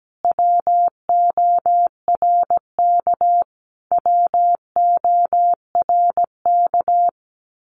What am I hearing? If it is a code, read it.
Morse code, heard as a keyed tone, about 17 wpm, 700 Hz. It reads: WORK WORK
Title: Top 500 Words Repeated 17wpm